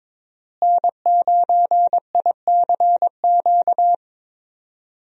Morse code, heard as a keyed tone, about 21 wpm, 700 Hz. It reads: N9ICQ
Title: USA Call Signs - 5 Characters - Repeated 2x - 22wpm